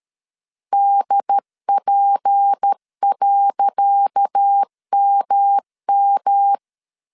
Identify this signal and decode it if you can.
Morse code, heard as a keyed tone, about 13 wpm, 800 Hz. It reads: DP.MM
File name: LEZIONI / 10 SIGNS / 10C DOT 12wpm 2451